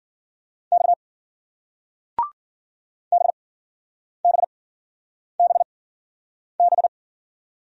Morse code, heard as a keyed tone, about 54 wpm, spaced as for 9 wpm, 700 Hz. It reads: X BBBB